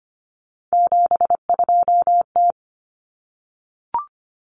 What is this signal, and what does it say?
Morse code, heard as a keyed tone, about 25 wpm, 700 Hz. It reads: 72T
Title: Common Cut Numbers - 3-Digit 25wpm